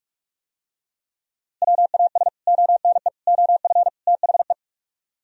Morse code, heard as a keyed tone, about 45 wpm, 700 Hz. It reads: WAS ONE OF THE